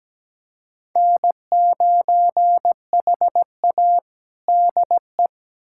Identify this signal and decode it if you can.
Morse code, heard as a keyed tone, about 17 wpm, 700 Hz. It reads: N9HA DE
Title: USA Callsign DE Callsign - 4 Characters - 17wpm